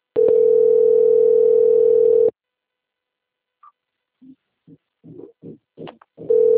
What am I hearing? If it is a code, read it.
Ringback tone (North America), 440 + 480 Hz.